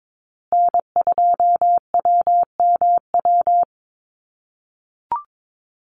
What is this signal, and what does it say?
Morse code, heard as a keyed tone, about 22 wpm, 700 Hz. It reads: N2WMW
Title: USA Call Signs Repeated 2x 22wpm